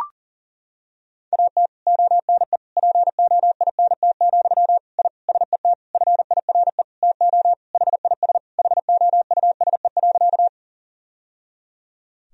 Morse code, heard as a keyed tone, about 40 wpm, 700 Hz. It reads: AT ONE POINT, I SET FIRE TO HIS HOUSE.
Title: Sentences from Top 1000 Words - Review Part 1 - 40wpm